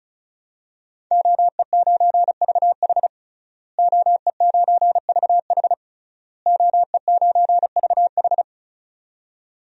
Morse code, heard as a keyed tone, about 34 wpm, 700 Hz. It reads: OE9VH OE9VH OE9VH
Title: DXCC-100 Call Signs - 5 Characters - Repeated 3x - 35wpm